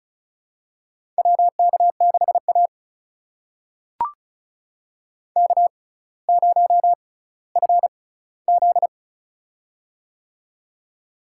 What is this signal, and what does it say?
Morse code, heard as a keyed tone, about 36 wpm, spaced as for 18 wpm, 700 Hz. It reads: WK6A K0FZ
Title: USA Call Signs - 4 Characters - 6x Character Spacing - 35wpm